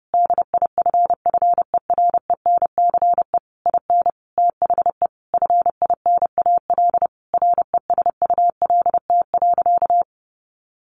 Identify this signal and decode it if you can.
Morse code, heard as a keyed tone, about 30 wpm, 700 Hz. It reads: DIFFERENCE IN THE FINAL RESULT.
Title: Sentences from Top 800 Words 30wpm